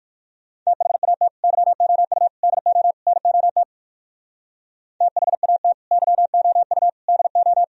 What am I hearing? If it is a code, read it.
Morse code, heard as a keyed tone, about 58 wpm, 700 Hz. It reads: THAT YOU DO NOT THAT YOU DO